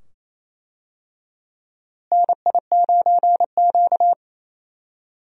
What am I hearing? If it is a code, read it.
Morse code, heard as a keyed tone, about 28 wpm, 700 Hz. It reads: NI9Q
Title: USA Call Signs - 4 Characters - Repeated 2x - 28wpm